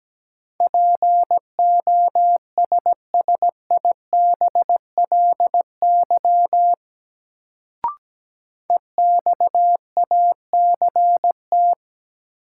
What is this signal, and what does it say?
Morse code, heard as a keyed tone, about 17 wpm, 700 Hz. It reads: POSSIBLY EXACT
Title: Sentences from Top 1300 Words - New Word Drill - 17wpm